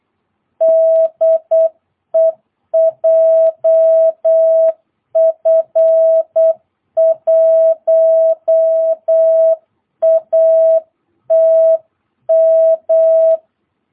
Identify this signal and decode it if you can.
Morse code, heard as a keyed tone, about 8 wpm, 660 Hz. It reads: DE JF1ATM
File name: jh8cbh / 2022cw